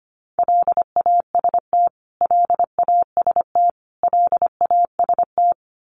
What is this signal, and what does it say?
Morse code, heard as a keyed tone, about 25 wpm, 700 Hz. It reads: LAST LAST LAST
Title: Top 300 Words Repeated 25wpm